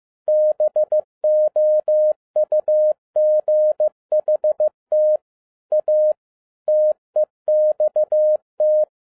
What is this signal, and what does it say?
Morse code, heard as a keyed tone, about 15 wpm, 610 Hz. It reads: BOUGHT A TEXT